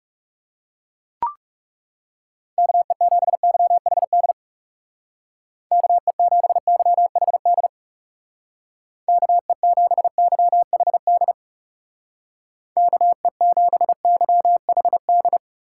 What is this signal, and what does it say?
Morse code, heard as a keyed tone, about 40 wpm, 700 Hz. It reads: KE7YHD KE7YHD KE7YHD KE7YHD